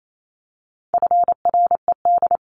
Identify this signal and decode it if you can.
Morse code, heard as a keyed tone, about 28 wpm, 700 Hz. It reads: FRED
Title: CWT Exchange 28wpm